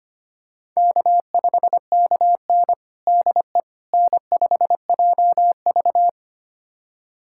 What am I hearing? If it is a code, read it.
Morse code, heard as a keyed tone, about 25 wpm, 700 Hz. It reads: K5KN DE N5JV
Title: USA Callsign DE Callsign - 4 Characters - 25wpm